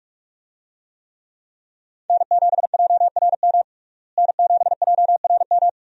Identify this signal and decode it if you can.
Morse code, heard as a keyed tone, about 41 wpm, 700 Hz. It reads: N7JRM N7JRM
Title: USA Call Signs - 5 Characters - Repeated 2x - 45wpm